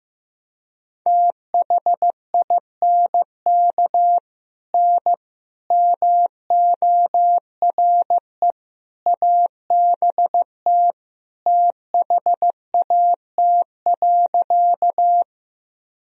Morse code, heard as a keyed tone, about 15 wpm, 700 Hz. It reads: THINK N MORE ABT THAT.
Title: Sentences from Top 200 Words and Abbreviations 15wpm